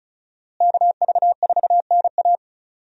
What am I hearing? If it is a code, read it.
Morse code, heard as a keyed tone, about 34 wpm, 700 Hz. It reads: KV4NA
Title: USA Call Signs - 5 Characters - Repeated 2x - 35wpm